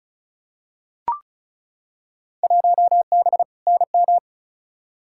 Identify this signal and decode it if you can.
Morse code, heard as a keyed tone, about 36 wpm, 700 Hz. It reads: 1B NM
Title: Field Day Exchanges Encore - 35wpm